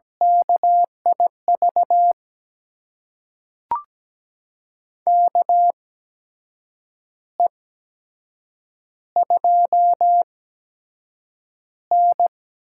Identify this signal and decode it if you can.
Morse code, heard as a keyed tone, about 17 wpm, spaced as for 7 wpm, 700 Hz. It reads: KIV KE2N
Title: USA Call Signs - 8x Character Spacing - 17wpm